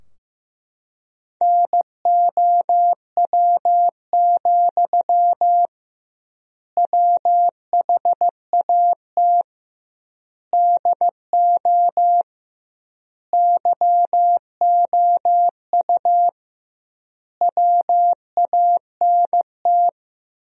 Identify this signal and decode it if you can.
Morse code, heard as a keyed tone, about 15 wpm, 700 Hz. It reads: NOW, WHAT DO YOU WANT
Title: 2x Word-Spacing - Sentences from Top 200 Words - 15wpm